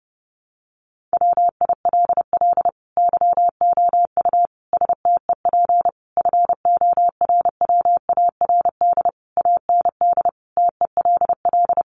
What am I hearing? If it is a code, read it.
Morse code, heard as a keyed tone, about 28 wpm, 700 Hz. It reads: WILL YOU STEP FORWARD AND TELL
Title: Sentences from Top 1300 Words 30wpm